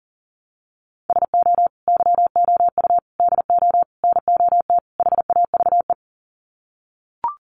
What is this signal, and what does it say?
Morse code, heard as a keyed tone, about 37 wpm, 700 Hz. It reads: SO YOU DO NOT HAVE
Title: Sets of 5 Words Encore 40wpm